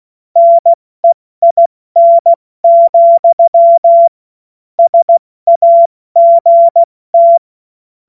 Morse code, heard as a keyed tone, about 16 wpm, 680 Hz. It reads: NEIN, SAGT
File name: Morsekurs / Witze / Witz 24 Juni Mai 2025 70 F80